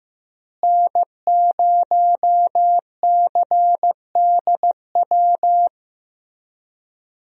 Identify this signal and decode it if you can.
Morse code, heard as a keyed tone, about 15 wpm, 700 Hz. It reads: N0CDW
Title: USA Call Signs - 5 Characters - Repeated 2x - 15wpm